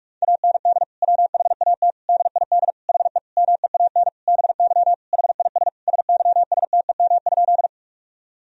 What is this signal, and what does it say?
Morse code, heard as a keyed tone, about 46 wpm, 700 Hz. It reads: AND WHAT DID HE MEAN BY HIS SYSTEM?
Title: Sentences from Top 1000 Words - Review Part 3 - 45wpm